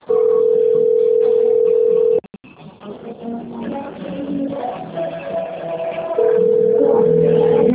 Ringback tone (North America), 440 + 480 Hz, with noise.
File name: out- - - - - .90876